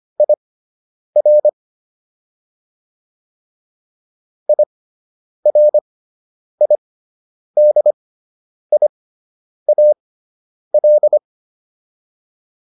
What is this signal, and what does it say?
Morse code, heard as a keyed tone, about 25 wpm, spaced as for 9 wpm, 600 Hz. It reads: IR IRIDIAL